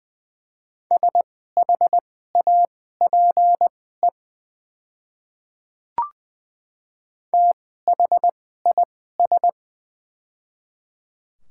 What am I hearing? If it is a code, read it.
Morse code, heard as a keyed tone, about 20 wpm, spaced as for 14 wpm, 700 Hz. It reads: SHAPE THIS
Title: 2x Character Spacing - Top 500 Words - 20wpm